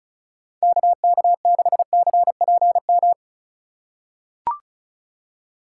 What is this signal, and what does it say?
Morse code, heard as a keyed tone, about 34 wpm, 700 Hz. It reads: KK6CPM
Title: USA Call Signs - 6 Characters - 3x Character Spacing - 35wpm